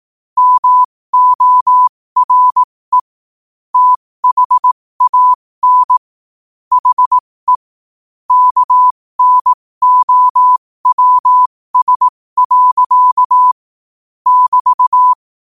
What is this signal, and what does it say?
Morse code, heard as a keyed tone, about 18 wpm, spaced as for 15 wpm, 1000 Hz. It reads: MORE THAN HE KNOWS. <BT>
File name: qotdfiles / 15wpm